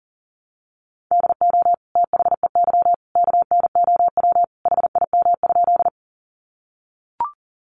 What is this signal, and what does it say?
Morse code, heard as a keyed tone, about 40 wpm, 700 Hz. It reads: DO THEY KNOW HIM?
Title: Sentences from Top 100 Words 40wpm